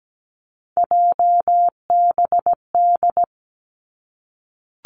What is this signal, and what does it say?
Morse code, heard as a keyed tone, about 17 wpm, 700 Hz. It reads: JBD
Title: Three Challenging Letter-Number Combinations 17wpm